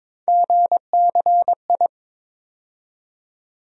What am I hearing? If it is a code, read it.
Morse code, heard as a keyed tone, about 21 wpm, 700 Hz. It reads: GCI
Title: USA Call Sign Postfixes - 3 Letters 22wpm